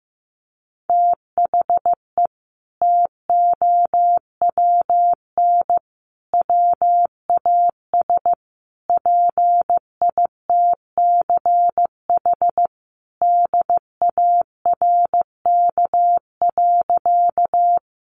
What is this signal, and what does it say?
Morse code, heard as a keyed tone, about 15 wpm, 700 Hz. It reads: THE TOWN WAS PITCH DARK.
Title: Sentences from Top 1300 Words 15wpm